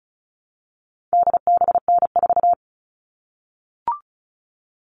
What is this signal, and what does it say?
Morse code, heard as a keyed tone, about 34 wpm, 700 Hz. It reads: DBN4
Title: All Cut Numbers - 4-Digit 35wpm